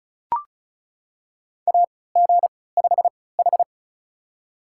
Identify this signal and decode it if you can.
Morse code, heard as a keyed tone, about 36 wpm, spaced as for 20 wpm, 700 Hz. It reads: AG5H
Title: USA Call Signs - 4 Characters - 3x Character Spacing - 35wpm